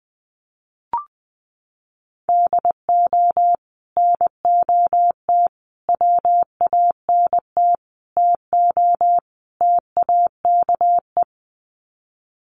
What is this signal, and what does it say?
Morse code, heard as a keyed tone, about 20 wpm, 700 Hz. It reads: DO NOT WANT TO TAKE
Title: Sets of 5 Words Encore2 20wpm